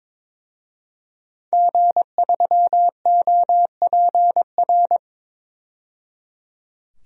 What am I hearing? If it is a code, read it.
Morse code, heard as a keyed tone, about 22 wpm, 700 Hz. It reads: G3OPR
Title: DXCC-100 Call Signs - 5 Characters - 22wpm